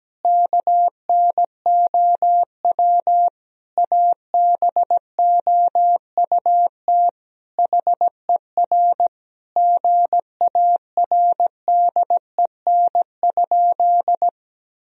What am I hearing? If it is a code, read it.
Morse code, heard as a keyed tone, about 17 wpm, 700 Hz. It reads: KNOW ABOUT HER GARDEN?
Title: Sentences from Top 1000 Words - Review Part 4 - 17wpm